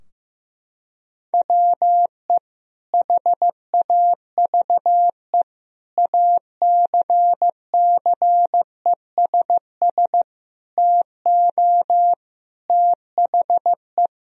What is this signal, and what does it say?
Morse code, heard as a keyed tone, about 15 wpm, 700 Hz. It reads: WE HAVE ACCESS TO THE
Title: Sentences from Top 1000 Words - Review Part 4 - 15wpm